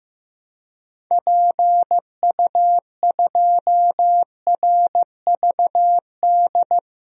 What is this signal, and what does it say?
Morse code, heard as a keyed tone, about 15 wpm, 700 Hz. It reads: PU2RVD
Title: DXCC-100 Call Signs - 6 Characters - Repeated 2x - 15wpm